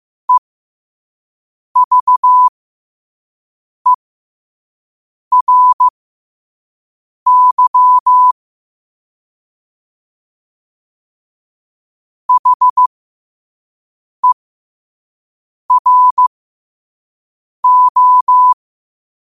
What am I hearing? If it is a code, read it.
Morse code, heard as a keyed tone, about 15 wpm, spaced as for 5 wpm, 1000 Hz. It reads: EVERY HERO
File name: qotdfiles / 5wpm